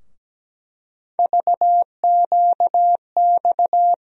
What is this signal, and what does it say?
Morse code, heard as a keyed tone, about 17 wpm, 700 Hz. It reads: VQX
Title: USA Call Sign Postfixes - 3 Letters 17wpm